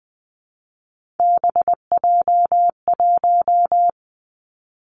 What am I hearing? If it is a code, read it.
Morse code, heard as a keyed tone, about 20 wpm, 700 Hz. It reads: BJ1
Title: Three Challenging Letter-Number Combinations 20wpm